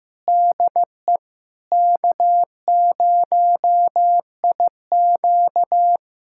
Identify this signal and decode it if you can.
Morse code, heard as a keyed tone, about 16 wpm, 700 Hz. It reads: DE K0IQ
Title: USA Callsign DE Callsign - 4 Characters - 15wpm